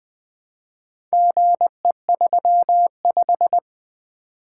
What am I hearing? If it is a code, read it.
Morse code, heard as a keyed tone, about 20 wpm, 700 Hz. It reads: GE35
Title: Four Letter-Number Combinations - 20wpm